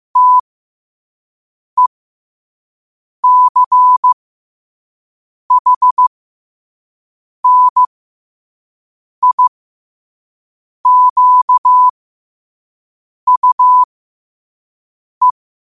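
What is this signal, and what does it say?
Morse code, heard as a keyed tone, about 15 wpm, spaced as for 5 wpm, 1000 Hz. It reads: TECHNIQUE
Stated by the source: Morse code practice at 5 wpm.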